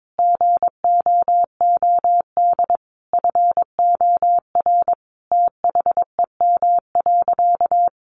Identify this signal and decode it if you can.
Morse code, heard as a keyed tone, about 22 wpm, 700 Hz. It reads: GOOD FOR THEM.
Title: Sentences from Top 1300 Words 22wpm